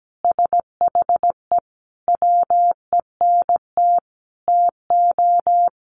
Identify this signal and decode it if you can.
Morse code, heard as a keyed tone, about 17 wpm, 700 Hz. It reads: SHE WENT TO
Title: Sentences from Top 1500 Words 17wpm